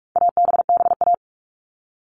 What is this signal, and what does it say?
Morse code, heard as a keyed tone, about 44 wpm, 700 Hz. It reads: ABBA
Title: All Cut Numbers - 4-Digit 45wpm